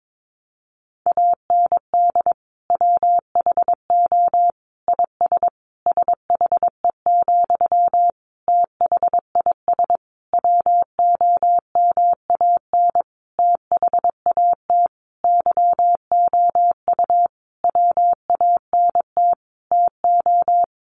Morse code, heard as a keyed tone, about 22 wpm, 700 Hz. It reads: AND WHO IS SHE, THIS WOMAN THAT YOU WANT TO
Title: Sentences from Top 1100 Words 22wpm